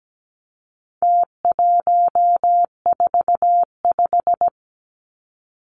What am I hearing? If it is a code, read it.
Morse code, heard as a keyed tone, about 17 wpm, 700 Hz. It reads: T145
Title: Numbers - Contest Style - 4-Digit 17wpm